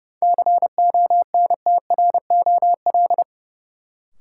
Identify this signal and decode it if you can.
Morse code, heard as a keyed tone, about 32 wpm, 700 Hz. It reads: CONTROL
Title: Sentences from Top 1000 Words - Review Part 2 - New Word Drill - 30wpm